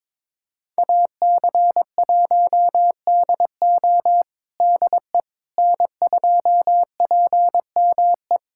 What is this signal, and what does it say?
Morse code, heard as a keyed tone, about 22 wpm, 700 Hz. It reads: AC1DO DE N2PME
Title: USA Callsign DE Callsign - 5 Characters - 22wpm